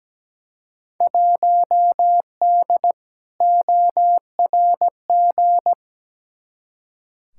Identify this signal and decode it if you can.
Morse code, heard as a keyed tone, about 17 wpm, 700 Hz. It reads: 1D ORG
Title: Field Day Exchanges - 17wpm